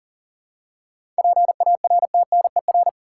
Morse code, heard as a keyed tone, about 40 wpm, 700 Hz. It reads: PARTNER
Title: Seven Letter Words - 40wpm